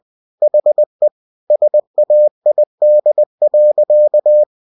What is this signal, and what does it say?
Morse code, heard as a keyed tone, about 20 wpm, 600 Hz. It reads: HE SAID.
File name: cw-books / winnie-the-pooh / winnie-20wpm-page-0044